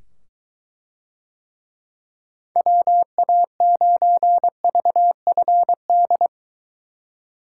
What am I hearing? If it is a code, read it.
Morse code, heard as a keyed tone, about 23 wpm, 700 Hz. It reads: WA9VFD